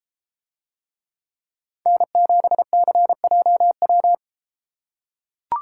Morse code, heard as a keyed tone, about 33 wpm, 700 Hz. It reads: N7CJW